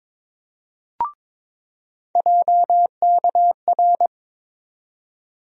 Morse code, heard as a keyed tone, about 22 wpm, 700 Hz. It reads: JKR
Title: USA Call Sign Postfixes - 3 Letters 22wpm